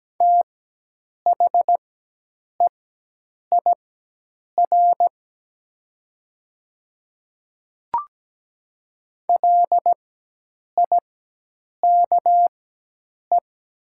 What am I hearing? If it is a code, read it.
Morse code, heard as a keyed tone, about 17 wpm, spaced as for 8 wpm, 700 Hz. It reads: THEIR LIKE
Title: 4x Character Spacing - Top 100 Words - 17wpm